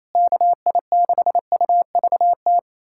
Morse code, heard as a keyed tone, about 28 wpm, 700 Hz. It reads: KI6UVT
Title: USA Callsign DE Callsign - 6 Characters - 28wpm